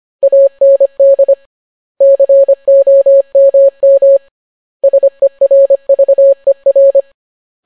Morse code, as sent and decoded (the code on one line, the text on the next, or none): .- -. -.. / -.-. --- -- -- / ... .-. ...- . .-.
AND COMM SERVER